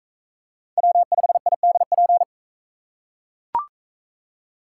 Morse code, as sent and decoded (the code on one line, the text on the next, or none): .-- ..... .. -.. .--.
W5IDP